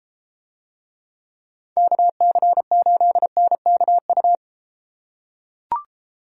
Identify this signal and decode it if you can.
Morse code, heard as a keyed tone, about 33 wpm, 700 Hz. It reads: KC8NKU